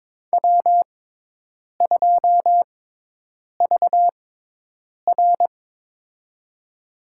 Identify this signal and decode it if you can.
Morse code, heard as a keyed tone, about 22 wpm, spaced as for 8 wpm, 700 Hz. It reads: W2VR
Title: USA Call Signs - 4 Characters - 6x Character Spacing - 22wpm